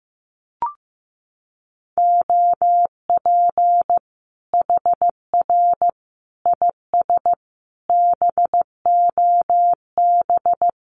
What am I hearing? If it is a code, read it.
Morse code, heard as a keyed tone, about 15 wpm, 700 Hz. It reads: OP HR IS BOB
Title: Common QSO Phrases 15wpm